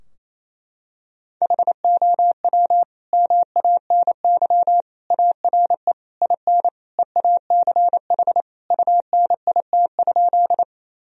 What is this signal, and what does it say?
Morse code, heard as a keyed tone, about 28 wpm, 700 Hz. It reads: HOW MANY ARE IN EACH UNIT?
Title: Sentences from Top 1000 Words - Review Part 2 - 28wpm